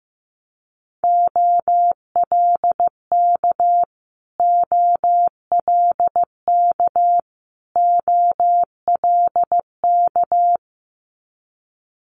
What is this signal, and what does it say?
Morse code, heard as a keyed tone, about 16 wpm, 700 Hz. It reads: OLK OLK OLK
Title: Multi-letter Phonemes Repeated 3x 15wpm